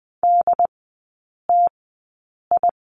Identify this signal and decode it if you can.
Morse code, heard as a keyed tone, about 20 wpm, spaced as for 8 wpm, 690 Hz. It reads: DTI
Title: 212 Lesson 34e - Number 8 - Three Characters 20wpm